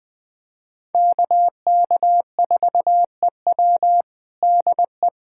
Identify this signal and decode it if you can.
Morse code, heard as a keyed tone, about 21 wpm, 700 Hz. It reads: KK4EW DE
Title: USA Callsign DE Callsign - 5 Characters - 20wpm